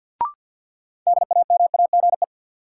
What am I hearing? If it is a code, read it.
Morse code, heard as a keyed tone, about 52 wpm, 700 Hz. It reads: DAMAGE
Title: Sentences from Top 1100 Words - New Word Drill Alt - 50wpm